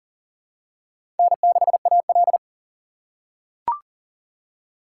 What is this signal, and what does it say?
Morse code, heard as a keyed tone, about 40 wpm, 700 Hz. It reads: N6AL
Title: USA Call Signs - 4 Characters - 4x Character Spacing - 40wpm